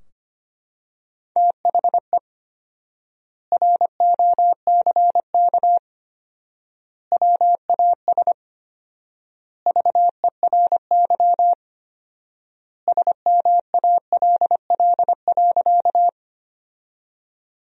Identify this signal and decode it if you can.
Morse code, heard as a keyed tone, about 25 wpm, 700 Hz. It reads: THE ROCK WAS VERY SMALL.
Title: 4x Word-Spacing - Sentences from Top 400 Words - 25wpm